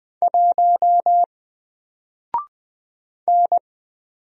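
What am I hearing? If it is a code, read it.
Morse code, heard as a keyed tone, about 20 wpm, 700 Hz. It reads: 1 N
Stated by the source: Single letter and number Morse drill at 20 wpm, from N4LCD.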